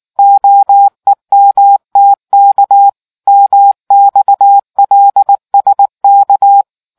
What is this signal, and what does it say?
Morse code, heard as a keyed tone, about 19 wpm, 800 Hz. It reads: OEMTK MXLSK